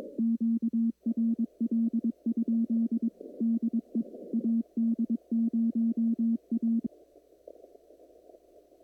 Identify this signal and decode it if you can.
Morse code, heard as a keyed tone, about 22 wpm, 230 Hz. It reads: QRL? DE AD0A
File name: CW / kt5j wpx cw-op-k5pi